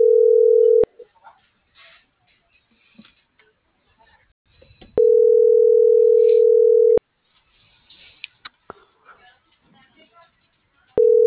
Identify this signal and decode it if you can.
Ringback tone (North America), 440 + 480 Hz.
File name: CENTRAL BLN / out marco.llanos